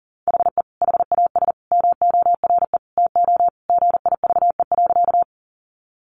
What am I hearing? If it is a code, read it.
Morse code, heard as a keyed tone, about 40 wpm, 700 Hz. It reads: HE HAS MORE TO GIVE.
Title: Sentences from Top 200 Words 40wpm